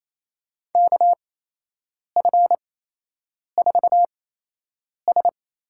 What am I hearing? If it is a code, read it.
Morse code, heard as a keyed tone, about 28 wpm, spaced as for 8 wpm, 700 Hz. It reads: KF4S
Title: USA Call Signs - 6 Characters - 8x Character Spacing - 28wpm